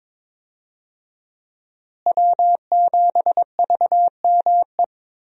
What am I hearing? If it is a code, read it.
Morse code, heard as a keyed tone, about 22 wpm, 700 Hz. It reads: W7VME
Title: USA Call Signs - 5 Characters - Repeated 3x - 22wpm